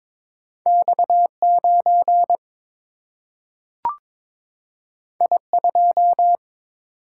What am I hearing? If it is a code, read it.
Morse code, heard as a keyed tone, about 22 wpm, 700 Hz. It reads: X 9 I 2
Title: Single-Letter Single-Number 22wpm